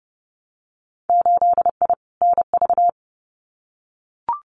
Morse code, heard as a keyed tone, about 32 wpm, 700 Hz. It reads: 8I NV
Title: Winter Field Day Exchanges Encore 30wpm